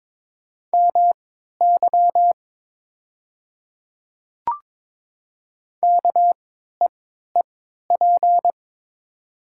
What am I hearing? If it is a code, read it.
Morse code, heard as a keyed tone, about 22 wpm, spaced as for 13 wpm, 690 Hz. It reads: MY KEEP
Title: 3x Character Spacing - Top 500 Words - 22wpm